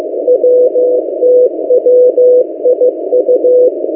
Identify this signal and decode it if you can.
Morse code, heard as a keyed tone, about 15 wpm, 500 Hz. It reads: WTWIU
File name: M08a- -1355utc-8096khz